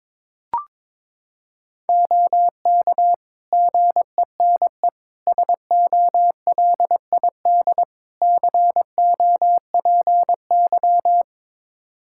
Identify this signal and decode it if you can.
Morse code, heard as a keyed tone, about 22 wpm, 700 Hz. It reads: OK GENE SOLID COPY
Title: Practice QSO - Exchange 3 - More Info - 22wpm